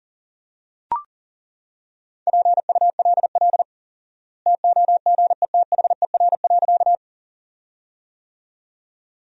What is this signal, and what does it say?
Morse code, heard as a keyed tone, about 37 wpm, 700 Hz. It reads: PULL TOGETHER.
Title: 4x Word-Spacing - Sentences from Top 400 Words - 40wpm